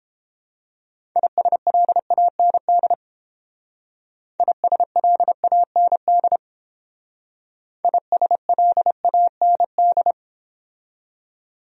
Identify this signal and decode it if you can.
Morse code, heard as a keyed tone, about 30 wpm, 700 Hz. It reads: ISLAND ISLAND ISLAND